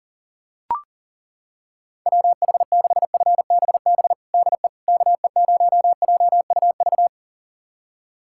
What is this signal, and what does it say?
Morse code, heard as a keyed tone, about 38 wpm, 700 Hz. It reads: WH6FBB DE KE0JUV
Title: USA Callsign DE Callsign 40wpm